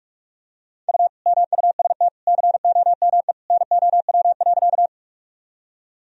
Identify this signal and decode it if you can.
Morse code, heard as a keyed tone, about 46 wpm, 700 Hz. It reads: U MUST COME NOW.
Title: Sentences from Top 200 Words and Abbreviations 45wpm